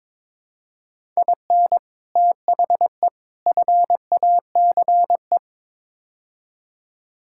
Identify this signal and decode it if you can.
Morse code, heard as a keyed tone, about 22 wpm, 700 Hz. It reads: IN THE FACE
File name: practice / 22wpm / Build A Phrase - 5-Word Phrases - 22wpm